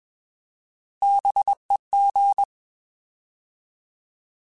Morse code, heard as a keyed tone, about 21 wpm, 800 Hz. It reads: BEG
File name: Bs @20